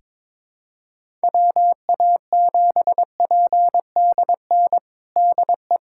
Morse code, heard as a keyed tone, about 22 wpm, 700 Hz. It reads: WA7PDN DE
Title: USA Callsign DE Callsign 22wpm